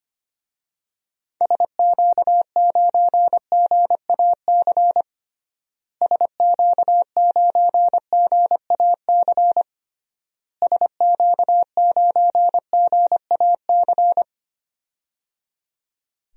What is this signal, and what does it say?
Morse code, heard as a keyed tone, about 25 wpm, 700 Hz. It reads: SQ9GAC SQ9GAC SQ9GAC
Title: DXCC-100 Call Signs - 6 Characters - Repeated 3x - 25wpm